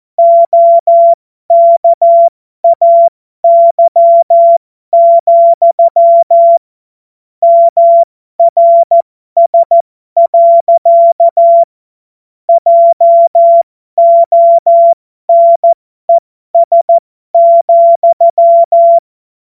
Morse code, heard as a keyed tone, about 14 wpm, spaced as for 12 wpm, 680 Hz. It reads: OKAY, MRS. JONES,
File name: Morsekurs / Witze